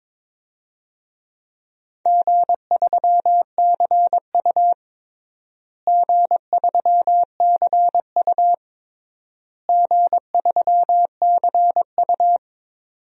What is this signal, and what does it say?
Morse code, heard as a keyed tone, about 22 wpm, 700 Hz. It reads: G3CU G3CU G3CU
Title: DXCC-100 Call Signs - 4 Characters - Repeated 3x - 22wpm